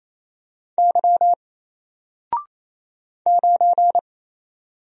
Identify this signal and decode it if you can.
Morse code, heard as a keyed tone, about 27 wpm, 700 Hz. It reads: Y 9
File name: Single-Letter-Number-28wpm